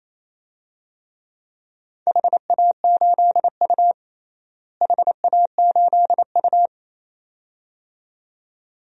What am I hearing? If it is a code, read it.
Morse code, heard as a keyed tone, about 28 wpm, 700 Hz. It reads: HA8U HA8U
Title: DXCC-100 Call Signs - 4 Characters - Repeated 2x - 28wpm